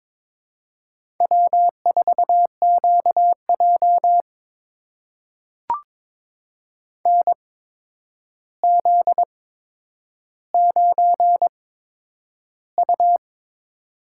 Morse code, heard as a keyed tone, about 22 wpm, spaced as for 9 wpm, 700 Hz. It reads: W4QJ NZ9U
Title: USA Call Signs - 8x Character Spacing - 22wpm